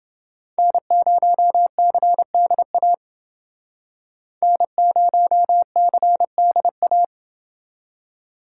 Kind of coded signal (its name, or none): Morse code